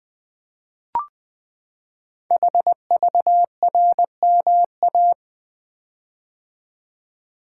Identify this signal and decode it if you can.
Morse code, heard as a keyed tone, about 20 wpm, 700 Hz. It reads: HVRMA
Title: Code Groups - 5 Letters - 20wpm